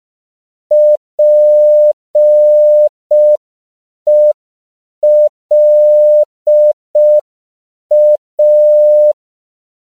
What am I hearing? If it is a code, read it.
Morse code, heard as a keyed tone, about 5 wpm, 600 Hz. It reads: PELA
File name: CW / 5PPM